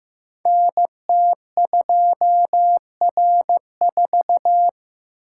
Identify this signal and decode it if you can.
Morse code, heard as a keyed tone, about 15 wpm, 700 Hz. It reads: NT2R4
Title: Code Groups - 5 Letter:Numbers - 15wpm